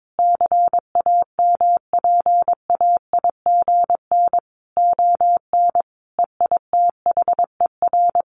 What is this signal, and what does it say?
Morse code, heard as a keyed tone, about 22 wpm, 700 Hz. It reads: CAMPAIGN ON EITHER
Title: Sentences from Top 1200 Words 22wpm